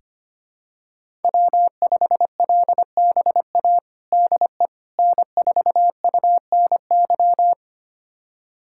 Morse code, heard as a keyed tone, about 25 wpm, 700 Hz. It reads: W5LBA DE N4UNY
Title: USA Callsign DE Callsign - 5 Characters - 25wpm